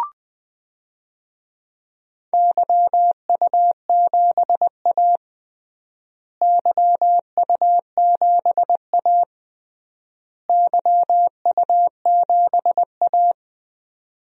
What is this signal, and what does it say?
Morse code, heard as a keyed tone, about 20 wpm, 700 Hz. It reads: YU7A YU7A YU7A
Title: DXCC-100 Call Signs - 4 Characters - Repeated 3x - 20wpm